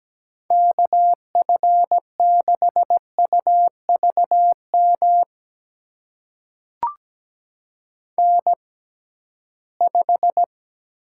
Morse code, heard as a keyed tone, about 17 wpm, spaced as for 13 wpm, 700 Hz. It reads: KF6UVM N5
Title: USA Call Signs - 6x Character Spacing - 17wpm